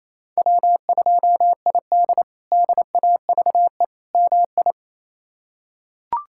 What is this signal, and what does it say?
Morse code, heard as a keyed tone, about 28 wpm, 700 Hz. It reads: W2ID DAVE MI
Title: K1USN SST QSO Practice - 28wpm